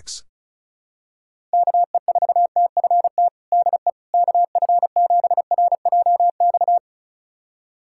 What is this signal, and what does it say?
Morse code, heard as a keyed tone, about 34 wpm, 700 Hz. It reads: KE4TFT DE KF7RJX
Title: USA Callsign DE Callsign - 6 Characters - 35wpm